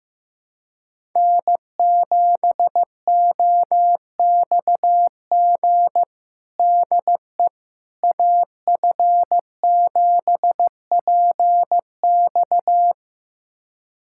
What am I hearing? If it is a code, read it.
Morse code, heard as a keyed tone, about 16 wpm, 700 Hz. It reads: N7OXG DE AF7PX
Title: USA Callsign DE Callsign - 5 Characters - 15wpm